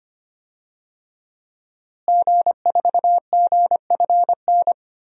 Morse code, heard as a keyed tone, about 25 wpm, 700 Hz. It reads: G4GFN
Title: DXCC-100 Call Signs - 5 Characters - Repeated 3x - 25wpm